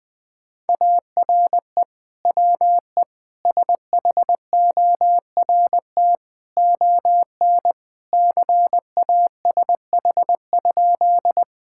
Morse code, heard as a keyed tone, about 20 wpm, 700 Hz. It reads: ARE WE SHORT ON CASH?
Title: Sentences from Top 1000 Words - Review Part 5 - 20wpm